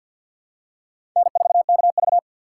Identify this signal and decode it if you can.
Morse code, heard as a keyed tone, about 48 wpm, 700 Hz. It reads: N4KV
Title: USA Call Signs - 4x Character Spacing - 50wpm